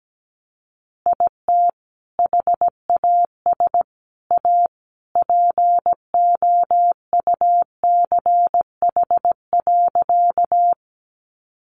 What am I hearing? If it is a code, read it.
Morse code, heard as a keyed tone, about 17 wpm, 700 Hz. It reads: IT HAS A POUCH.
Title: Easy Phrases 17wpm